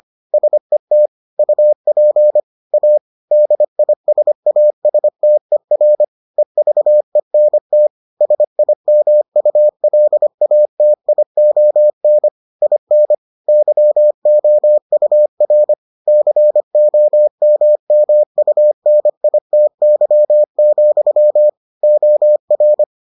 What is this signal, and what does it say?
Morse code, heard as a keyed tone, about 25 wpm, 600 Hz. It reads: SET UP A DISASTER EVENT SIMULATION IN YOUR COMMUNITY, OR